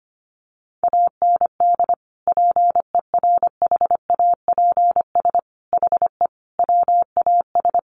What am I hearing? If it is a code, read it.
Morse code, heard as a keyed tone, about 25 wpm, 700 Hz. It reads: AND PERHAPS HE WAS
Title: Sentences from Top 900 Words 25wpm